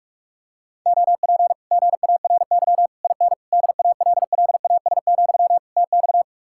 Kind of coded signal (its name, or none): Morse code